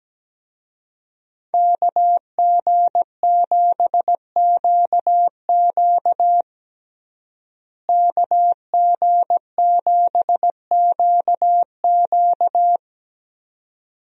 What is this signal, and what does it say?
Morse code, heard as a keyed tone, about 17 wpm, 700 Hz. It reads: KG7QQ KG7QQ
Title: USA Call Signs - 5 Characters - Repeated 2x - 17wpm